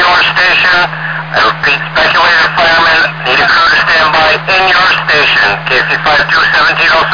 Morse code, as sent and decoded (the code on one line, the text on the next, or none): none